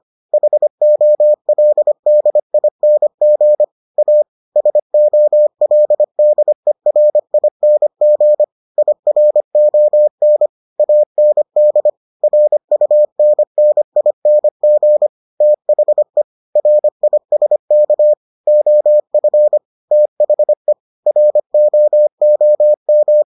Morse code, heard as a keyed tone, about 25 wpm, 600 Hz. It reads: HOLDING A SOLDERING IRON AND RUNNING THE RISK OF THE ROOM